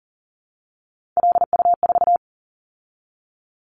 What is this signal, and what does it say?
Morse code, heard as a keyed tone, about 40 wpm, 700 Hz. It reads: LU4
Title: Callsign Prefixes for Top 50 DXCC - 40wpm